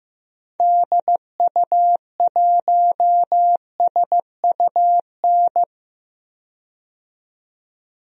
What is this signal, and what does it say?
Morse code, heard as a keyed tone, about 15 wpm, 700 Hz. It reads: DU1SUN
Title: DXCC-100 Call Signs - 6 Characters - 15wpm